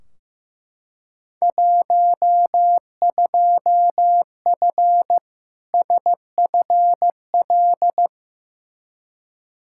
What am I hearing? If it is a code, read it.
Morse code, heard as a keyed tone, about 15 wpm, 700 Hz. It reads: 12F SFL
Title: Field Day Exchanges Encore - 15wpm